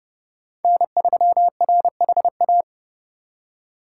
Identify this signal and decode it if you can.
Morse code, heard as a keyed tone, about 30 wpm, 700 Hz. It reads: N3RHA